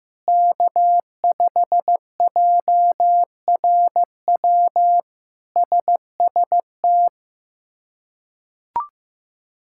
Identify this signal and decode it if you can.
Morse code, heard as a keyed tone, about 14 wpm, 700 Hz. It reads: K5JRW SST
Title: K1USN SST QSO Practice - 15wpm